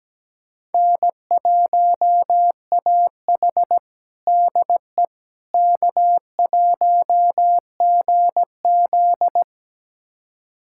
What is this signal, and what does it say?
Morse code, heard as a keyed tone, about 17 wpm, 700 Hz. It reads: N1AH DE K1GZ
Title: USA Callsign DE Callsign - 4 Characters - 17wpm